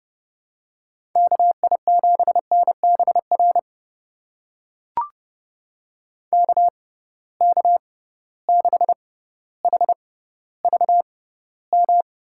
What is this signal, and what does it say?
Morse code, heard as a keyed tone, about 30 wpm, spaced as for 15 wpm, 700 Hz. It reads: KI7NBR KK6HVM
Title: USA Call Signs - 6 Characters - 6x Character Spacing - 30wpm